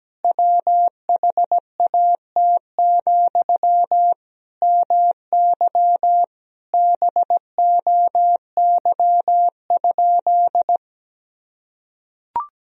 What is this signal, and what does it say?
Morse code, heard as a keyed tone, about 17 wpm, 700 Hz. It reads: WHAT, MY BOY?
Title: Sentences from Top 1000 Words - Review Part 4 - 17wpm